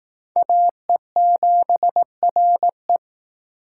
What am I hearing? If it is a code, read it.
Morse code, heard as a keyed tone, about 18 wpm, 700 Hz. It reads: AE7RE
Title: Speed Racing - USA 2x2 Call Signs 15wpm